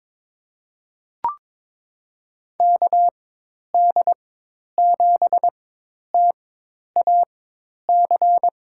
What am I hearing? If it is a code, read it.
Morse code, heard as a keyed tone, about 22 wpm, spaced as for 10 wpm, 700 Hz. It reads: KD7TAC
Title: USA Call Signs - 4x Character Spacing - 22wpm